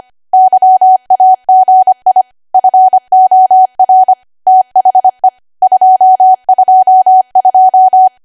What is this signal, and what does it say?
Morse code, heard as a keyed tone, about 25 wpm, 750 Hz. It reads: YAGI FOR THE 222
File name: Archive / 25 WPM / 25WPM